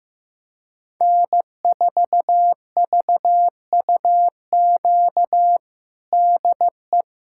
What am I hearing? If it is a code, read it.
Morse code, heard as a keyed tone, about 15 wpm, 700 Hz. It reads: N4VUQ DE
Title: USA Callsign DE Callsign 15wpm